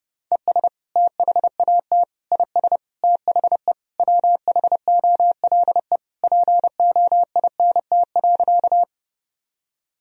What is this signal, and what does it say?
Morse code, heard as a keyed tone, about 32 wpm, 700 Hz. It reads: ES THAT IS THE WHOLE POINT.
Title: Sentences from Top 400 Words and Abbreviations 30wpm